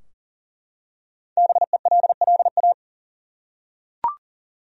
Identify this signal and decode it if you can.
Morse code, heard as a keyed tone, about 40 wpm, 700 Hz. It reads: BELLA
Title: USA Common Names - Max 5 Letters - 40wpm